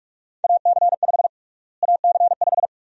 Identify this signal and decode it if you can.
Morse code, heard as a keyed tone, about 44 wpm, 700 Hz. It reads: AC5 AC5
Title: USA Call Sign Prefixes - Repeated 2x - 45wpm